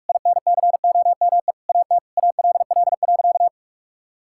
Morse code, heard as a keyed tone, about 46 wpm, 700 Hz. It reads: INCOME AT ALL.
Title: Sentences from Top 1000 Words - Review Part 5 - 45wpm